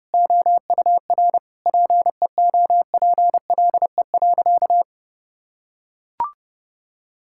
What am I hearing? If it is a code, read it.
Morse code, heard as a keyed tone, about 30 wpm, 700 Hz. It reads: OUR PEOPLE.
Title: Sentences from Top 1000 Words - Review Part 5 - 30wpm